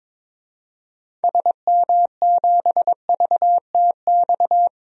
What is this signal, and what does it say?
Morse code, heard as a keyed tone, about 22 wpm, 700 Hz. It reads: SM7VTX
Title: DXCC-100 Call Signs - 6 Characters - 22wpm